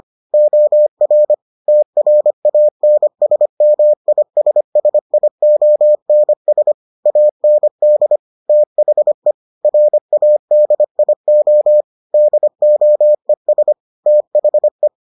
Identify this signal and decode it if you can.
Morse code, heard as a keyed tone, about 25 wpm, 600 Hz. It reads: OR TRANSMISSIONS AND THE RADIO DOES THE